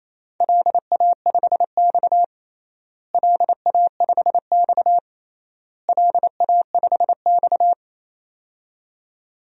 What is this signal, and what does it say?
Morse code, heard as a keyed tone, about 28 wpm, 700 Hz. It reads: LA5X LA5X LA5X
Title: DXCC-100 Call Signs - 4 Characters - Repeated 3x - 28wpm